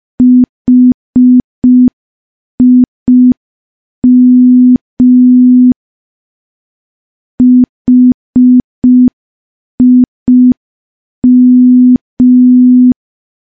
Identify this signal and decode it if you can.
Morse code, heard as a keyed tone, about 5 wpm, 250 Hz. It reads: HIM HIM
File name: morse / morse-training-05WPM